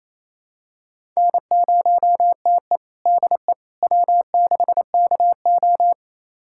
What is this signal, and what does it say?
Morse code, heard as a keyed tone, about 28 wpm, 700 Hz. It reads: N0TE DE W6KO
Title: USA Callsign DE Callsign - 4 Characters - 28wpm